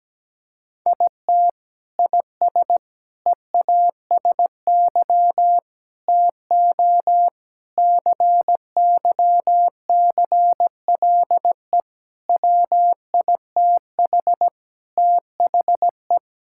Morse code, heard as a keyed tone, about 17 wpm, 700 Hz. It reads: IT IS EASY TO CYCLE WITH THE
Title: Sentences from Top 1000 Words - Review Part 6 - 17wpm